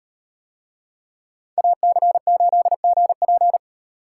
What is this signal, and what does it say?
Morse code, heard as a keyed tone, about 35 wpm, 700 Hz. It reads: AC8GP